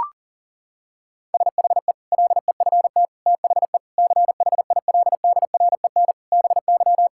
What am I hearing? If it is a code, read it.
Morse code, heard as a keyed tone, about 37 wpm, 700 Hz. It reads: SHE LEFT THE CHILDREN BY
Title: Sentences from Top 1000 Words - Review Part 5 - 40wpm